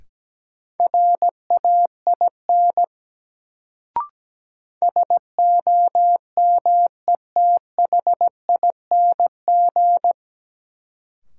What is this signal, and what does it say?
Morse code, heard as a keyed tone, about 17 wpm, 700 Hz. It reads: RAIN SOMETHING
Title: Sentences from Top 600 Words - New Word Drill - 17wpm